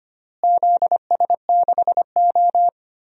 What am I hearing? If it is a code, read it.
Morse code, heard as a keyed tone, about 25 wpm, 700 Hz. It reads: ZS6O
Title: DXCC-100 Call Signs - 4 Characters - Repeated 3x - 25wpm